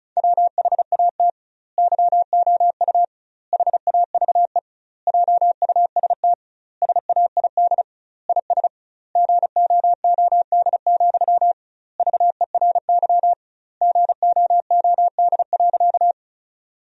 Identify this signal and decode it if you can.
Morse code, heard as a keyed tone, about 34 wpm, 700 Hz. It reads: WHAT YOU HAVE JUST SAID IS GOOD, VERY GOOD.
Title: 2x Word-Spacing - Sentences from Top 200 Words - 35wpm